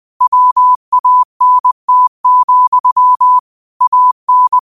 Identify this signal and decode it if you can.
Morse code, heard as a keyed tone, about 20 wpm, 1000 Hz. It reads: WANT, AN